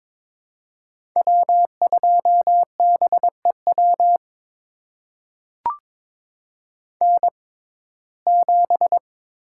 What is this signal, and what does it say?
Morse code, heard as a keyed tone, about 22 wpm, spaced as for 16 wpm, 700 Hz. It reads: W2BEW N7
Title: USA Call Signs - 5 Characters - 6x Character Spacing - 22wpm